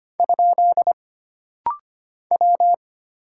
Morse code, heard as a keyed tone, about 25 wpm, 700 Hz. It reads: ? W
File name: Single Character (Letter:Number:Punctuation:Prosign) - Rapid-Fire - 25wpm